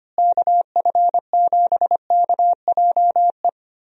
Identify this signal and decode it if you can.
Morse code, heard as a keyed tone, about 25 wpm, 700 Hz. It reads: KF7KJE
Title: USA Call Signs - 6 Characters - Repeated 2x - 25wpm